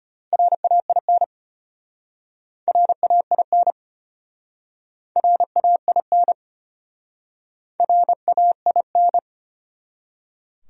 Morse code, heard as a keyed tone, about 35 wpm, 700 Hz. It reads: RAIN RAIN RAIN RAIN